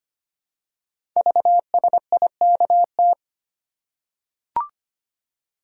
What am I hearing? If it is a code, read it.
Morse code, heard as a keyed tone, about 25 wpm, 700 Hz. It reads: VSIKT
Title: Code Groups - 5 Letters - 25wpm